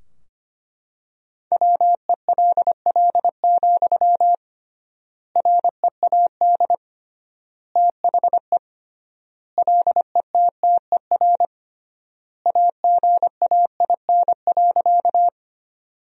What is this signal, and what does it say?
Morse code, heard as a keyed tone, about 25 wpm, 700 Hz. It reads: WELL, READ THE LETTER AGAIN.
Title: 3x Word-Spacing - Sentences from Top 300 Words - 25wpm